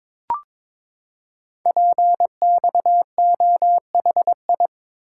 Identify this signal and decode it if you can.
Morse code, heard as a keyed tone, about 23 wpm, 700 Hz. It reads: PXOHI
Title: Code Groups - 5 Letters - 22wpm